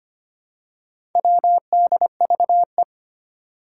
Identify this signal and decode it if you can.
Morse code, heard as a keyed tone, about 25 wpm, 700 Hz. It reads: WDVE
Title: Four Challenging Letter-Number Combinations 25wpm